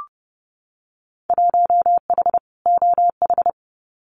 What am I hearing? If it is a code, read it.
Morse code, heard as a keyed tone, about 30 wpm, 700 Hz. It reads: 1H OH
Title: Winter Field Day Exchanges 30wpm